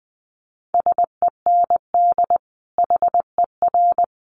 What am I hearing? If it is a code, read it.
Morse code, heard as a keyed tone, about 20 wpm, 700 Hz. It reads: SEND HER
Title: Sentences from Top 1500 Words 20wpm